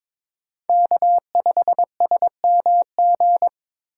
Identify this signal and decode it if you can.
Morse code, heard as a keyed tone, about 22 wpm, 700 Hz. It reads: K5SMG
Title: USA Call Signs - 5 Characters - 4x Character Spacing - 22wpm